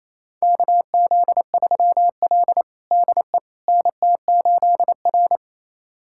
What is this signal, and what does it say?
Morse code, heard as a keyed tone, about 28 wpm, 700 Hz. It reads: KZ3L DE NT8R
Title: USA Callsign DE Callsign - 4 Characters - 28wpm